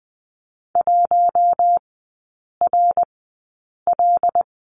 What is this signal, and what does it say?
Morse code, heard as a keyed tone, about 20 wpm, spaced as for 8 wpm, 700 Hz. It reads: 1RL
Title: 178 Lesson 29e - Slash - Three Characters 20wpm